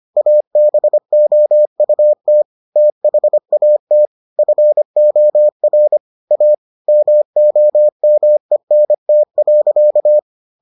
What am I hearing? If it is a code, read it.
Morse code, heard as a keyed tone, about 25 wpm, 600 Hz. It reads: ABOUT THAT FOR A MOMENT.